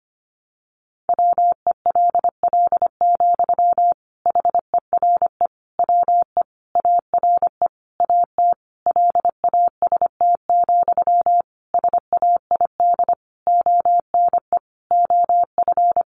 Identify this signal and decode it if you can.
Morse code, heard as a keyed tone, about 25 wpm, 700 Hz. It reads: WELL, HERE WE ARE AT LAST, SAID ONE OF
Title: Sentences from Top 300 Words 25wpm